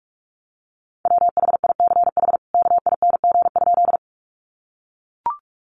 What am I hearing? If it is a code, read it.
Morse code, heard as a keyed tone, about 44 wpm, 700 Hz. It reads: WHICH KING?
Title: Sentences from Top 400 Words 45wpm